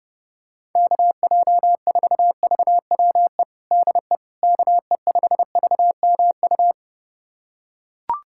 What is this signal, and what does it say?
Morse code, heard as a keyed tone, about 32 wpm, 700 Hz. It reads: KJ4VWE DE KE5VMU
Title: USA Callsign DE Callsign - 6 Characters - 30wpm